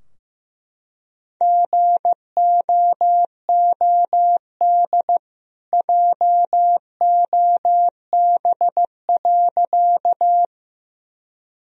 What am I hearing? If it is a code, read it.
Morse code, heard as a keyed tone, about 15 wpm, 700 Hz. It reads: GOOD JOB.
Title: Sentences from Top 1000 Words - Review Part 3 - 15wpm